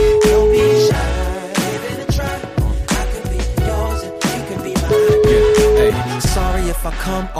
Ringback tone (Europe, 425 Hz), with noise.